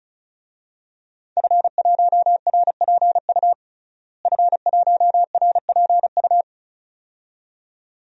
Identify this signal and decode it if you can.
Morse code, heard as a keyed tone, about 36 wpm, 700 Hz. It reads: F1RPU F1RPU
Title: DXCC-100 Call Signs - 5 Characters - Repeated 2x - 35wpm